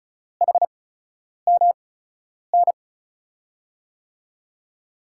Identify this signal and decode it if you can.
Morse code, heard as a keyed tone, about 36 wpm, spaced as for 10 wpm, 690 Hz. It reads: HMN
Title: USA Call Signs - 5 Characters - 8x Character Spacing - 35wpm